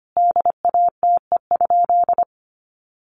Morse code, heard as a keyed tone, about 25 wpm, 700 Hz. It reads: DATE?
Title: Sentences from Top 1400 Words 25wpm